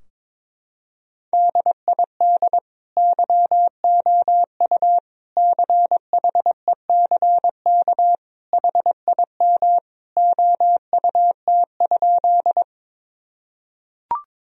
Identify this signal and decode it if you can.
Morse code, heard as a keyed tone, about 22 wpm, 700 Hz. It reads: DID YOU CHECK HIM OUT?
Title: Sentences from Top 1000 Words - Review Part 3 - 22wpm